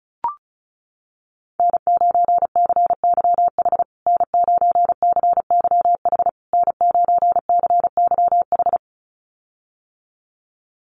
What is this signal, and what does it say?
Morse code, heard as a keyed tone, about 34 wpm, 700 Hz. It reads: N9CYH N9CYH N9CYH
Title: USA Call Signs Repeated 3x 35wpm